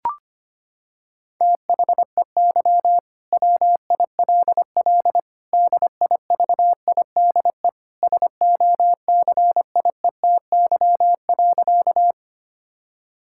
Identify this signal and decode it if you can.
Morse code, heard as a keyed tone, about 25 wpm, 700 Hz. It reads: THEY WILL DIVIDE SOCIETY.
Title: Sentences from Top 1600 Words 25wpm